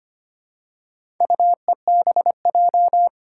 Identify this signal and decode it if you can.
Morse code, heard as a keyed tone, about 25 wpm, 700 Hz. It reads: UEBJ
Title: Four Challenging Letter-Number Combinations 25wpm